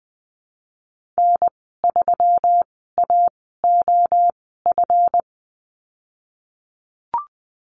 Morse code, heard as a keyed tone, about 20 wpm, spaced as for 15 wpm, 700 Hz. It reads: N3AOF
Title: 161 Lesson 26g - Letter g - Call Signs 20wpm